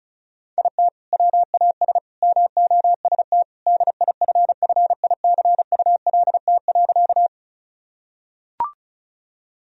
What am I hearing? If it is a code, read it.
Morse code, heard as a keyed tone, about 36 wpm, 700 Hz. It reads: IT WAS MOST DIFFICULT.
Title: Sentences from Top 1000 Words - Review Part 1 - 35wpm